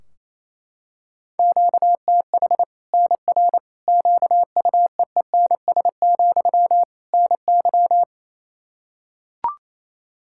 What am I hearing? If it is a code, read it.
Morse code, heard as a keyed tone, about 28 wpm, 700 Hz. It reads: QTH NR QUEENS, NY
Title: Practice QSO - Exchange 1 - Beginning - 28wpm